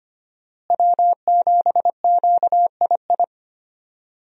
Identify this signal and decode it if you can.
Morse code, heard as a keyed tone, about 25 wpm, 700 Hz. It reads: W7QII
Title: USA Call Signs - 5 Characters - Repeated 2x - 25wpm